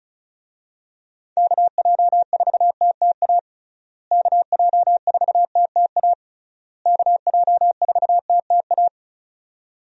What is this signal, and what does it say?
Morse code, heard as a keyed tone, about 36 wpm, 700 Hz. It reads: KJ4TTA KJ4TTA KJ4TTA
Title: USA Call Signs - 6 Characters - Repeated 3x - 35wpm